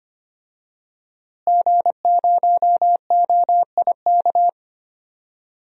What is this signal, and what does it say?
Morse code, heard as a keyed tone, about 25 wpm, 700 Hz. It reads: G0OIK
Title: DXCC-100 Call Signs - 5 Characters - Repeated 2x - 25wpm